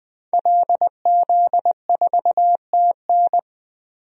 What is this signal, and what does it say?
Morse code, heard as a keyed tone, about 21 wpm, 700 Hz. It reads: LZ4TN
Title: DXCC-100 Call Signs - 5 Characters - Repeated 3x - 20wpm